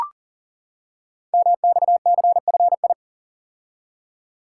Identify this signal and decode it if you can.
Morse code, heard as a keyed tone, about 40 wpm, 690 Hz. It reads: MXCFI
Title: Code Groups - 5 Letters - 40wpm